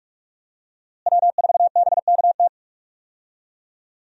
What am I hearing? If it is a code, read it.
Morse code, heard as a keyed tone, about 47 wpm, 700 Hz. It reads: W4BKT
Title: USA Call Signs - 5 Characters - Repeated 2x - 45wpm